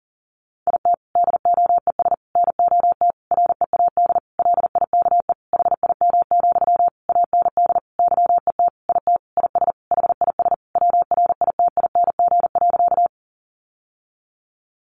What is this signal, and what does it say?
Morse code, heard as a keyed tone, about 40 wpm, 700 Hz. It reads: IT DOES NOT READ LIKE HIM, AND YET IT IS HIS WRITING.
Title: Sentences from Top 800 Words 40wpm